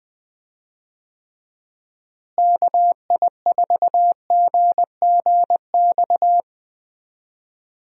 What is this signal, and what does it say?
Morse code, heard as a keyed tone, about 21 wpm, 700 Hz. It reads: KI4GGX
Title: USA Call Signs - 6 Characters - Repeated 3x - 20wpm